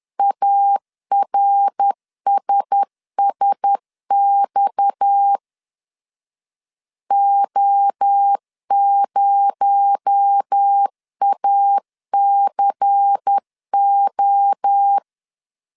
Morse code, heard as a keyed tone, about 10 wpm, 800 Hz. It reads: ARSSX O0ACO
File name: CW / DET / 1 6 1500C S B 10wpm 4137